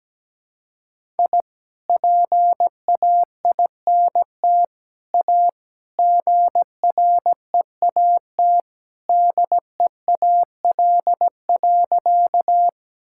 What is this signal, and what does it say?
Morse code, heard as a keyed tone, about 17 wpm, 700 Hz. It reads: I PAINT A GREAT DEAL.
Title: Sentences from Top 1000 Words - Review Part 6 - 17wpm